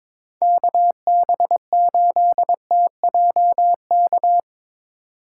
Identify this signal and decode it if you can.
Morse code, heard as a keyed tone, about 22 wpm, 700 Hz. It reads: KB8TJK
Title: USA Call Signs - 6 Characters - 2x Character Spacing - 22wpm